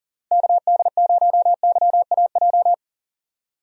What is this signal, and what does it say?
Morse code, heard as a keyed tone, about 40 wpm, 700 Hz. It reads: KD0YAJ
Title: USA Call Signs - 6 Characters - Repeated 3x - 40wpm